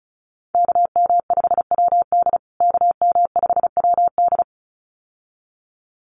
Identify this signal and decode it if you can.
Morse code, heard as a keyed tone, about 34 wpm, 700 Hz. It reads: KM5WD KM5WD
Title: USA Call Signs Repeated 2x 35wpm